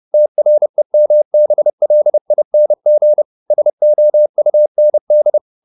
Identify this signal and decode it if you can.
Morse code, heard as a keyed tone, about 30 wpm, 600 Hz. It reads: TREMBLING SOUND
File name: the old man and the sea / the old man and the sea 1 30wpm